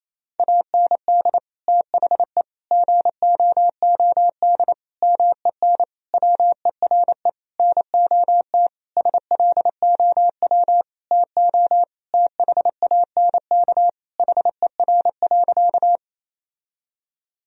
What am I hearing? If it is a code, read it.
Morse code, heard as a keyed tone, about 28 wpm, 700 Hz. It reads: AND THE GOOD MEN WERE NOT SLOW TO THANK HER.
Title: Sentences from Top 1000 Words - Review Part 2 - 28wpm